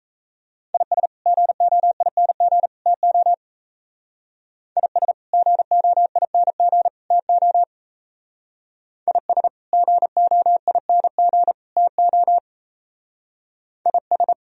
Morse code, heard as a keyed tone, about 35 wpm, 700 Hz. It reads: IS GOING TO IS GOING TO IS GOING TO IS
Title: Speed-Racing - Sets of 3 Words 28wpm